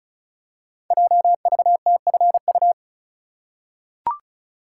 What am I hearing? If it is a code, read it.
Morse code, heard as a keyed tone, about 36 wpm, 690 Hz. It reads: JVTFU
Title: Code Groups - 5 Letter:Numbers - 35wpm